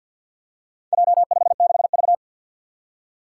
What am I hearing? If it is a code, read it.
Morse code, heard as a keyed tone, about 53 wpm, 700 Hz. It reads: J56V
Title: Four Challenging Letter-Number Combinations 50wpm